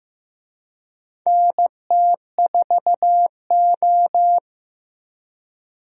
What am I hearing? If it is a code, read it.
Morse code, heard as a keyed tone, about 15 wpm, 700 Hz. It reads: NT4O
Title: USA Call Signs - 4 Characters - Repeated 2x - 15wpm